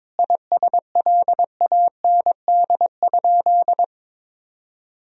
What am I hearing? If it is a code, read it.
Morse code, heard as a keyed tone, about 22 wpm, 700 Hz. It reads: ISLAND?
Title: Sentences from Top 1000 Words - Review Part 3 - 22wpm